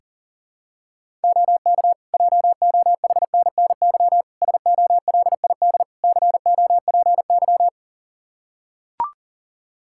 Morse code, heard as a keyed tone, about 40 wpm, 700 Hz. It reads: OK JOHNNY SOLID COPY
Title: Practice QSO - Exchange 3 - More Info - 40wpm